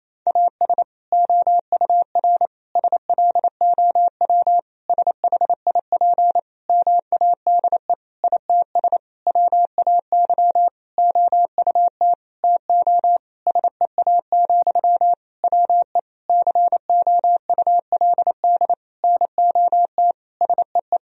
Morse code, heard as a keyed tone, about 27 wpm, 700 Hz. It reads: AS OUR SLOW SHIP MADE ITS WAY OUT TO SEA, WE COULD NOT SEE
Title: Sentences from Top 1000 Words - Review Part 2 - 28wpm